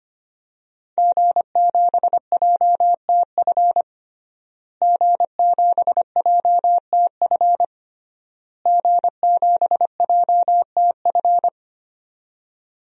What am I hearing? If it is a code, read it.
Morse code, heard as a keyed tone, about 25 wpm, 700 Hz. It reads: G7JTF G7JTF G7JTF
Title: DXCC-100 Call Signs - 5 Characters - Repeated 3x - 25wpm